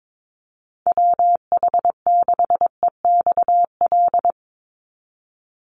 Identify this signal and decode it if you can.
Morse code, heard as a keyed tone, about 22 wpm, 700 Hz. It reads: WH6EXL
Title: USA Call Signs 22wpm